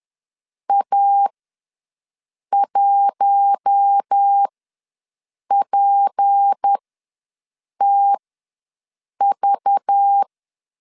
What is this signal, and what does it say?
Morse code, heard as a keyed tone, about 10 wpm, spaced as for 6 wpm, 800 Hz. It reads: A1PTV
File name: LEZIONI / 8 YJ45 / 8 YJ45 ws 10wpm 3843